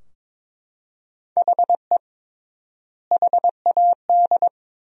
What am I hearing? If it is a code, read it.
Morse code, heard as a keyed tone, about 22 wpm, 700 Hz. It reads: HE HAD
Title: 3x Word-Spacing - Sentences from Top 300 Words - 22wpm